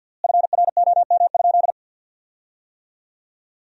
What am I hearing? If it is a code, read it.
Morse code, heard as a keyed tone, about 53 wpm, 700 Hz. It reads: FROM?
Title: 4x Word-Spacing - Sentences from Top 200 Words - 50wpm